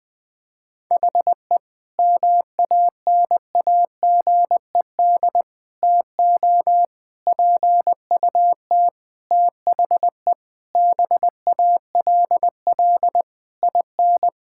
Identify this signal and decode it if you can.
Morse code, heard as a keyed tone, about 21 wpm, 700 Hz. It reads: HE MANAGED TO PUT THE BALL IN
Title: Sentences with 9-Word Phrases - 20wpm